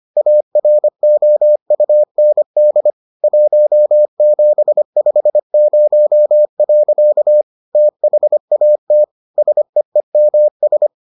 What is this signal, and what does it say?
Morse code, heard as a keyed tone, about 25 wpm, 600 Hz. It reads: AROUND 1750. THAT SEEMS